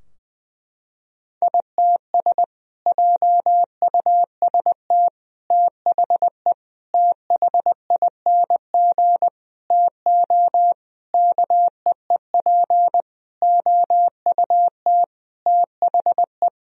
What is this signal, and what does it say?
Morse code, heard as a keyed tone, about 20 wpm, 700 Hz. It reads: ITS JUST THE THING TO KEEP OUT THE
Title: Sentences from Top 1000 Words - Review Part 2 - 20wpm